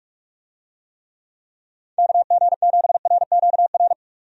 Morse code, heard as a keyed tone, about 46 wpm, 700 Hz. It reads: KG7RQR
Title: USA Call Signs - 6 Characters - 45wpm